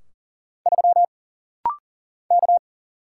Morse code, heard as a keyed tone, about 40 wpm, 700 Hz. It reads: 3 K
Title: Single Letter-Number - Rapid-Fire 40wpm